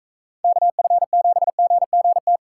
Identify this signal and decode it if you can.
Morse code, heard as a keyed tone, about 43 wpm, 700 Hz. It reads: KF7GGT